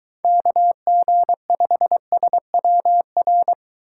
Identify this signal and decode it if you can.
Morse code, heard as a keyed tone, about 24 wpm, 700 Hz. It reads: KG5SWR